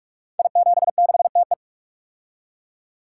Morse code, heard as a keyed tone, about 44 wpm, 700 Hz. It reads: I76TE
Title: Code Groups - 5 Letter:Numbers - 45wpm